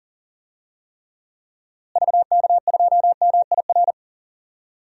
Morse code, heard as a keyed tone, about 40 wpm, 700 Hz. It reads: VK2MIR
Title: DXCC-100 Call Signs - 6 Characters - Repeated 2x - 40wpm